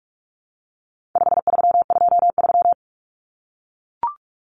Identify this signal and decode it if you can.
Morse code, heard as a keyed tone, about 44 wpm, 700 Hz. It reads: 5323
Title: Numbers - 4-Digit 45wpm